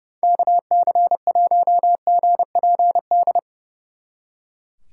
Morse code, heard as a keyed tone, about 36 wpm, 700 Hz. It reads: KC1GPD
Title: Speed Racing - USA 2x3 Call Signs 30wpm